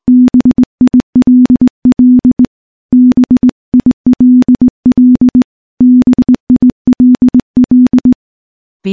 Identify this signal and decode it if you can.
Morse code, heard as a keyed tone, about 19 wpm, 250 Hz. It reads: BILL BILL BILL
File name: morse / morse-training-20WPM